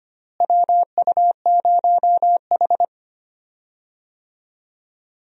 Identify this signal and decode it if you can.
Morse code, heard as a keyed tone, about 25 wpm, 700 Hz. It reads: WU0H
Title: USA Call Signs - 4 Characters - Repeated 2x - 25wpm